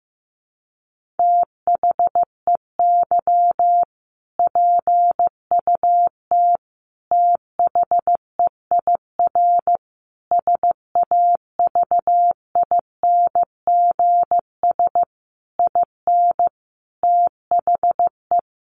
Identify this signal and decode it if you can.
Morse code, heard as a keyed tone, about 15 wpm, 700 Hz. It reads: THEY PUT THEIR SAVINGS IN THE
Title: Sentences from Top 1200 Words 15wpm